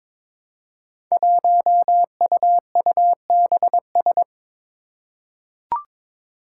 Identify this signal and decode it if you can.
Morse code, heard as a keyed tone, about 22 wpm, 700 Hz. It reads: 1UUBS
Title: Five Challenging Letter-Number Combinations 22wpm